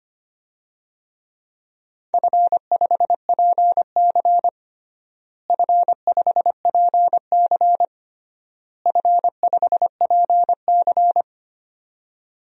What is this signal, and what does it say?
Morse code, heard as a keyed tone, about 25 wpm, 700 Hz. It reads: F5PC F5PC F5PC
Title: DXCC-100 Call Signs - 4 Characters - Repeated 3x - 25wpm